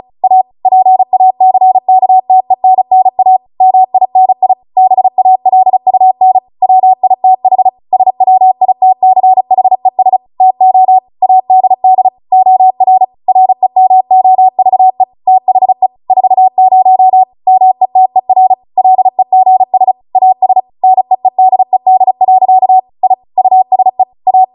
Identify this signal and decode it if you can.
Morse code, heard as a keyed tone, about 36 wpm, 750 Hz. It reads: A PACKTENNA MINI BALUN WITH SWITCHES TO ADD OR REMOVE THE 40 METER LEGS AS NEEDED. I USE A